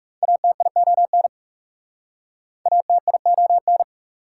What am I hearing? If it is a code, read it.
Morse code, heard as a keyed tone, about 37 wpm, 690 Hz. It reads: ATION ATION
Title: Speed-Racing Prefixes and Suffixes 30wpm